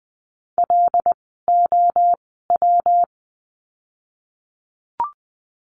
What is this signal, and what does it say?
Morse code, heard as a keyed tone, about 20 wpm, spaced as for 15 wpm, 700 Hz. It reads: LOW
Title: 115 Lesson 19f - Letter w - Words 20wpm